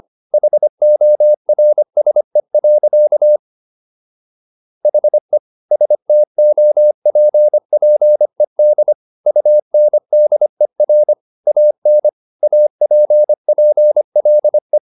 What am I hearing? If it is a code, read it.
Morse code, heard as a keyed tone, about 25 wpm, 600 Hz. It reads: HORSE. HE STOPPED UNDER AN APPLE